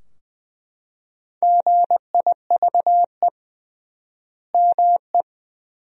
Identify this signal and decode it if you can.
Morse code, heard as a keyed tone, about 20 wpm, 700 Hz. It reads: GIVE ME
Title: 3x Word-Spacing - Sentences from Top 200 Words - 20wpm